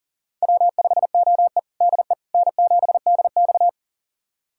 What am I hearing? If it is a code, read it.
Morse code, heard as a keyed tone, about 40 wpm, 700 Hz. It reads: W5OE DE N7DX
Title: USA Callsign DE Callsign - 4 Characters - 40wpm